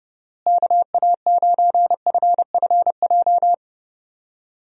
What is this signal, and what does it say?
Morse code, heard as a keyed tone, about 30 wpm, 700 Hz. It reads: KA9FFJ
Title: USA Call Signs - 6 Characters - Repeated 3x - 30wpm